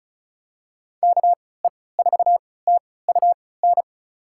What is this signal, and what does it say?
Morse code, heard as a keyed tone, about 34 wpm, spaced as for 20 wpm, 700 Hz. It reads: KE4TUN
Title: USA Call Signs - 6 Characters - 3x Character Spacing - 35wpm